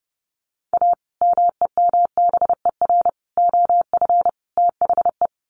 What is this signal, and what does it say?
Morse code, heard as a keyed tone, about 30 wpm, 700 Hz. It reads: A MEMBER OF THE
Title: Sets of 4 Words 30wpm